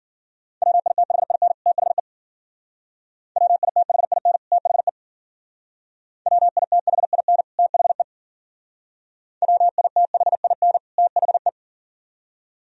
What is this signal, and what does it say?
Morse code, heard as a keyed tone, about 53 wpm, 690 Hz. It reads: WITHIN THE WITHIN THE WITHIN THE WITHIN THE